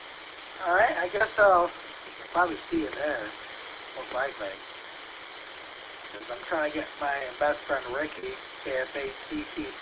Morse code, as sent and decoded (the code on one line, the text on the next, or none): none